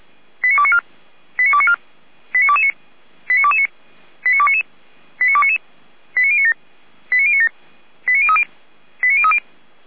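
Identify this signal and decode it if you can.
Five-tone selective call (ZVEI), heard as quick runs of five tones, tones of about 70 ms. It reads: 89294 89294 89229 89229 89299 89299 89097 89097 89030 89030
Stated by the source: ZVEI five-tone selective call received with an SDR.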